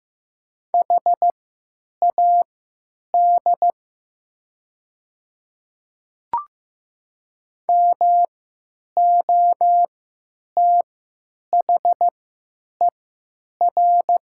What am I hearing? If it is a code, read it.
Morse code, heard as a keyed tone, about 15 wpm, spaced as for 9 wpm, 700 Hz. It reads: HAD MOTHER
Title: 3x Character Spacing - Top 200 Words - 15wpm